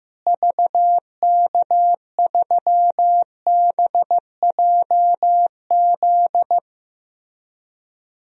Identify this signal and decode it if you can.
Morse code, heard as a keyed tone, about 15 wpm, 700 Hz. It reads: VK3BJZ
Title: DXCC-100 Call Signs - 6 Characters - Repeated 3x - 15wpm